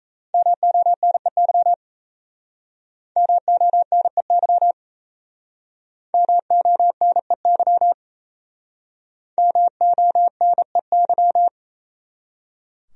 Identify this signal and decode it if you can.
Morse code, heard as a keyed tone, about 34 wpm, 700 Hz. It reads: MONEY MONEY MONEY MONEY
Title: Speed-Racing Top 500 Words 28wpm